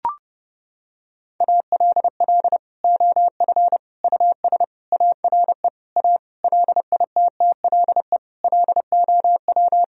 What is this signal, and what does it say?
Morse code, heard as a keyed tone, about 32 wpm, 700 Hz. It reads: ALL OF US ARE A LITTLE LOW
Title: Sentences from Top 1000 Words - Review Part 3 - 30wpm